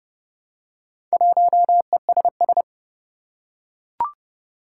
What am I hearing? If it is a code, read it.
Morse code, heard as a keyed tone, about 30 wpm, 700 Hz. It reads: 1ESS
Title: Four Challenging Letter-Number Combinations 30wpm